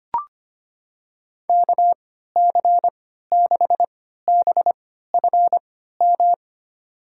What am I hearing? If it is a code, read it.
Morse code, heard as a keyed tone, about 25 wpm, spaced as for 14 wpm, 700 Hz. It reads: KC6BFM
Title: USA Call Signs - 6 Characters - 3x Character Spacing - 25wpm